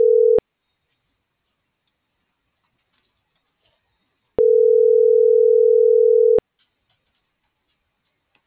Ringback tone (North America), 440 + 480 Hz.